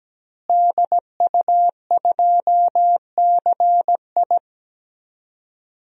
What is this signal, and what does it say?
Morse code, heard as a keyed tone, about 17 wpm, 700 Hz. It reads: DU2CI
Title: DXCC-100 Call Signs - 5 Characters - Repeated 3x - 17wpm